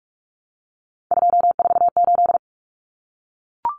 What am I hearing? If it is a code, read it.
Morse code, heard as a keyed tone, about 44 wpm, 700 Hz. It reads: 248
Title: Common Cut Numbers - 3-Digit 45wpm